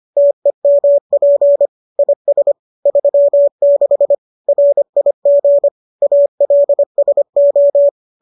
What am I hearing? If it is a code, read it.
Morse code, heard as a keyed tone, about 25 wpm, 570 Hz. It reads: TEMP IS 36 RIG ALSO